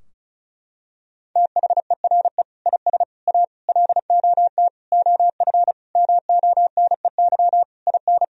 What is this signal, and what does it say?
Morse code, heard as a keyed tone, about 36 wpm, 700 Hz. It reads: THERE IS A LOT OF MONEY IN
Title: Sentences from Top 1000 Words - Review Part 4 - 35wpm